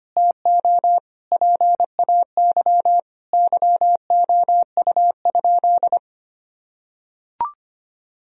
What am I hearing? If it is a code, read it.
Morse code, heard as a keyed tone, about 25 wpm, 700 Hz. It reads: TO PAY YOU?
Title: Sentences from Top 1000 Words - Review Part 3 - 25wpm